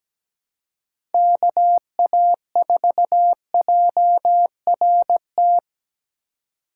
Morse code, heard as a keyed tone, about 20 wpm, 700 Hz. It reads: KA4JRT
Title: Speed Racing - USA 2x3 Call Signs 17wpm